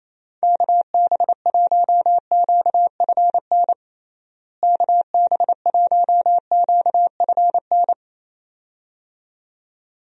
Morse code, heard as a keyed tone, about 28 wpm, 700 Hz. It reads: KB1QFN KB1QFN
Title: USA Call Signs - 6 Characters - Repeated 2x - 28wpm